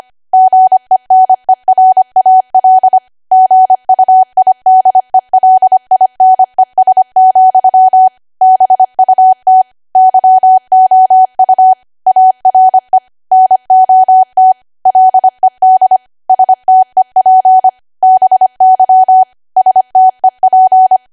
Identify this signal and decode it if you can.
Morse code, heard as a keyed tone, about 25 wpm, 750 Hz. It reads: GENERAL GUIDELINES, BUT YOU ARE NOT LED STEP BY STEP